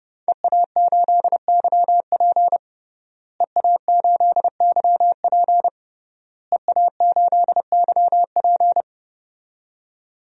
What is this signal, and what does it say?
Morse code, heard as a keyed tone, about 32 wpm, 700 Hz. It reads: EA8YP EA8YP EA8YP
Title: DXCC-100 Call Signs - 5 Characters - Repeated 3x - 30wpm